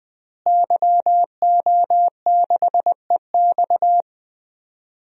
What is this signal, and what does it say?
Morse code, heard as a keyed tone, about 20 wpm, 700 Hz. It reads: YO6EX
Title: DXCC-100 Call Signs - 5 Characters - Repeated 3x - 20wpm